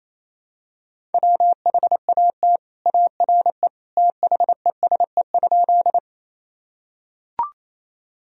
Morse code, heard as a keyed tone, about 28 wpm, 700 Hz. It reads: WHAT ARE THESE?
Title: Sentences from Top 1000 Words - Review Part 1 - 28wpm